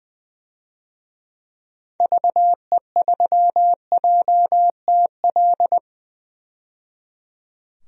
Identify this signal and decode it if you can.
Morse code, heard as a keyed tone, about 21 wpm, 700 Hz. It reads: VE3JTL
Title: DXCC-100 Call Signs - 6 Characters - 20wpm